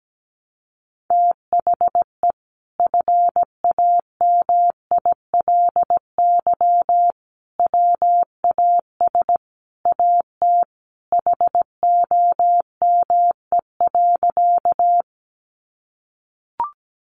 Morse code, heard as a keyed tone, about 17 wpm, 700 Hz. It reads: THE FAMILY WAS AT HOME.
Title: Sentences from Top 400 Words 17wpm